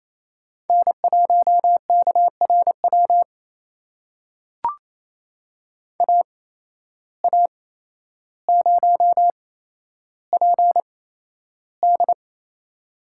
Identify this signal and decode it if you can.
Morse code, heard as a keyed tone, about 28 wpm, spaced as for 12 wpm, 700 Hz. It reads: N1KRW AA0PD
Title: USA Call Signs - 5 Characters - 8x Character Spacing - 28wpm